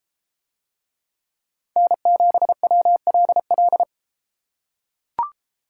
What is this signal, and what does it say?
Morse code, heard as a keyed tone, about 31 wpm, 700 Hz. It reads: N7WLL